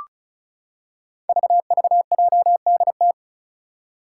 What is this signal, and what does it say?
Morse code, heard as a keyed tone, about 36 wpm, 700 Hz. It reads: VVJDT
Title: Five Challenging Letter-Number Combinations 35wpm